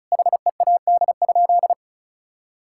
Morse code, heard as a keyed tone, about 36 wpm, 700 Hz. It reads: HEAD?
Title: Sentences from Top 1000 Words - Review Part 5 - 35wpm